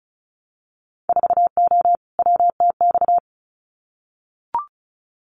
Morse code, heard as a keyed tone, about 36 wpm, 700 Hz. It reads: 4O WTX
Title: Winter Field Day Exchanges Encore 35wpm